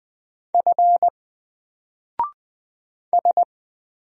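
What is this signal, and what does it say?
Morse code, heard as a keyed tone, about 20 wpm, 700 Hz. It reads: F S
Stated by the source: Single letter and number Morse drill at 20 wpm, from N4LCD.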